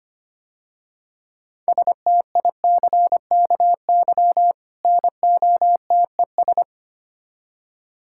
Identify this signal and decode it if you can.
Morse code, heard as a keyed tone, about 26 wpm, 700 Hz. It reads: STICKY NOTES
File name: Easy Words - Speed-Racing with Hints - 25wpm